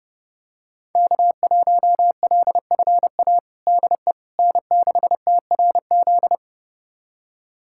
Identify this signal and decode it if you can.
Morse code, heard as a keyed tone, about 30 wpm, 700 Hz. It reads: K1LFA DE N6TRZ
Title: USA Callsign DE Callsign - 5 Characters - 30wpm